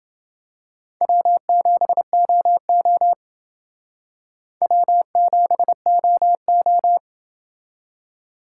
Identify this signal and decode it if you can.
Morse code, heard as a keyed tone, about 28 wpm, 700 Hz. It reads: W7OO W7OO